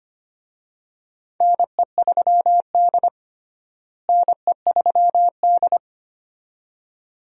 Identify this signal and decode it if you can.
Morse code, heard as a keyed tone, about 25 wpm, 700 Hz. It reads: NE3D NE3D
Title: USA Call Signs - 4 Characters - Repeated 2x - 25wpm